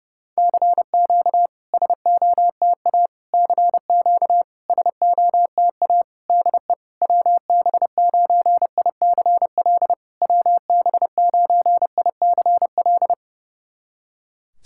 Morse code, heard as a keyed tone, about 30 wpm, 700 Hz. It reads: CQ SOTA CQ SOTA DE WB9ICL WB9ICL
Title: SOTA QSO Practice 30wpm